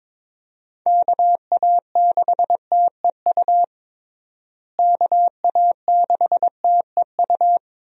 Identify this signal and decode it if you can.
Morse code, heard as a keyed tone, about 22 wpm, 700 Hz. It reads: KA6TEU KA6TEU
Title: USA Call Signs - 6 Characters - Repeated 2x - 22wpm